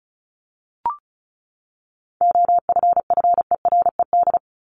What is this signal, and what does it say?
Morse code, heard as a keyed tone, about 34 wpm, 700 Hz. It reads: OFFERED
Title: ED Suffix - Top 100 Words with Base Words 35wpm